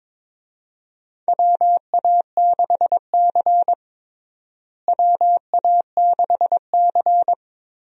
Morse code, heard as a keyed tone, about 21 wpm, 700 Hz. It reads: WA6C WA6C
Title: USA Call Signs - 4 Characters - Repeated 3x - 22wpm